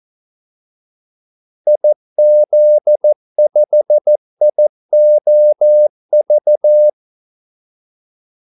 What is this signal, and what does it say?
Morse code, heard as a keyed tone, about 14 wpm, 600 Hz. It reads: IZ5IOV
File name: Callsigns-14wpm